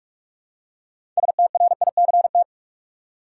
Morse code, heard as a keyed tone, about 43 wpm, 700 Hz. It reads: STRICT
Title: Six Letter Words - 45wpm